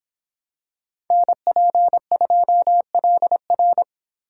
Morse code, heard as a keyed tone, about 26 wpm, 700 Hz. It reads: NP2LR